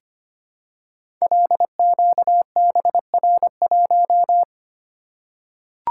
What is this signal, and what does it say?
Morse code, heard as a keyed tone, about 25 wpm, 700 Hz. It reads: LQBR1
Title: Code Groups - 5 Letter:Numbers - 25wpm